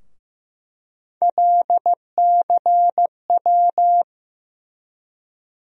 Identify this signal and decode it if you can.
Morse code, heard as a keyed tone, about 15 wpm, 700 Hz. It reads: LCW
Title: USA Call Sign Postfixes - 3 Letters 15wpm